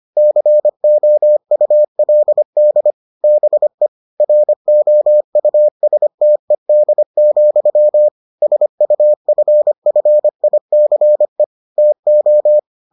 Morse code, heard as a keyed tone, about 25 wpm, 600 Hz. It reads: COULD BE ROUSTED, SUFFICE TO